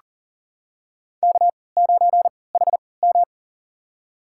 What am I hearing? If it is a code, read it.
Morse code, heard as a keyed tone, about 40 wpm, spaced as for 23 wpm, 700 Hz. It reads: K9HM
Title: USA Call Signs - 4 Characters - 3x Character Spacing - 40wpm